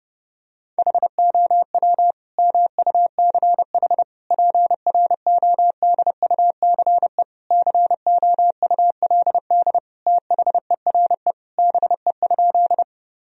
Morse code, heard as a keyed tone, about 30 wpm, 700 Hz. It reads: HOW MUCH PRODUCE COULD THERE BE?
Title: Sentences from Top 1000 Words - Review Part 3 - 30wpm